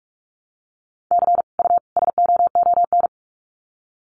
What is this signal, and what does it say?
Morse code, heard as a keyed tone, about 44 wpm, 700 Hz. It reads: C U SOON
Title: Common QSO Phrases 45wpm